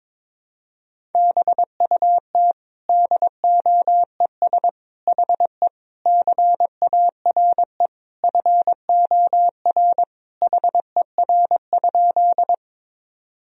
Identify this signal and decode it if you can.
Morse code, heard as a keyed tone, about 22 wpm, 700 Hz. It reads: BUT DOES HE CARE FOR HER?
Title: Sentences from Top 1000 Words - Review Part 1 - 22wpm